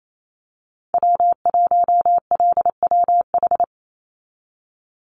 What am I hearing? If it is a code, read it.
Morse code, heard as a keyed tone, about 28 wpm, 700 Hz. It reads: W1LWH
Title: Call Signs Repeated 3x - 28wpm